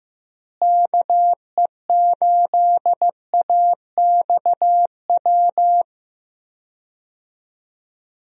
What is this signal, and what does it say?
Morse code, heard as a keyed tone, about 15 wpm, 700 Hz. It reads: KE8AXW
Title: USA Call Signs - 6 Characters - Repeated 3x - 15wpm